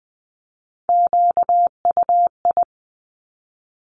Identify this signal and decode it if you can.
Morse code, heard as a keyed tone, about 20 wpm, 700 Hz. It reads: QUI
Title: Top Three Letter Combinations in Words 20wpm